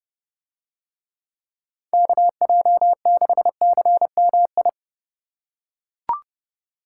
Morse code, heard as a keyed tone, about 29 wpm, 700 Hz. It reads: KJ6CMI